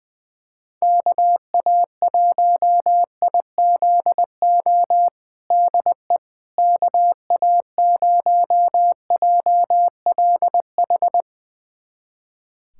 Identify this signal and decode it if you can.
Morse code, heard as a keyed tone, about 21 wpm, 700 Hz. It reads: KA1IZO DE KA0JLH
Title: USA Callsign DE Callsign - 6 Characters - 20wpm